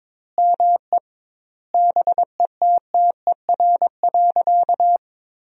Morse code, heard as a keyed tone, about 22 wpm, 700 Hz. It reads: ME BETTER.
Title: 2x Word-Spacing - Sentences from Top 400 Words - 22wpm